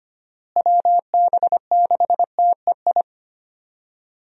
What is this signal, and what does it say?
Morse code, heard as a keyed tone, about 25 wpm, 700 Hz. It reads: WB6TEI
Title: USA Call Signs - 6 Characters - Repeated 2x - 25wpm